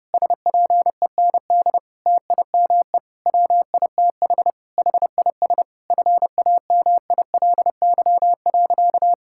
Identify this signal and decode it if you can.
Morse code, heard as a keyed tone, about 30 wpm, 700 Hz. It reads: SPEND TIME WITH HIS FAMILY.
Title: Sentences from Top 1000 Words - Review Part 6 - 30wpm